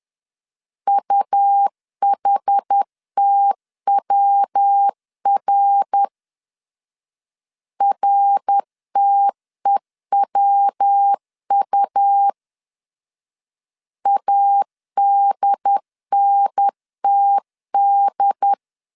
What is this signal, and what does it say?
Morse code, heard as a keyed tone, about 10 wpm, 800 Hz. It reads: UHTWR RTEWU ADNTD
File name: CW / DET / 1 3 1500C S C 10wpm 3335